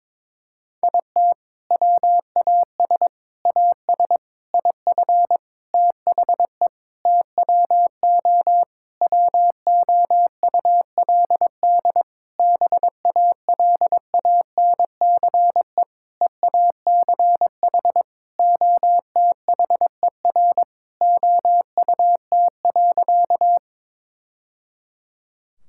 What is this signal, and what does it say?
Morse code, heard as a keyed tone, about 22 wpm, 700 Hz. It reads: IT WAS AS IF THE TWO WOULD BALANCE EACH OTHER OUT.
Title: Sentences from Top 1000 Words - Review Part 6 - 22wpm